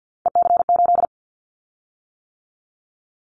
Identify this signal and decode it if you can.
Morse code, heard as a keyed tone, about 50 wpm, 700 Hz. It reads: EC8
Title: Callsign Prefixes for Top 50 DXCC - 50wpm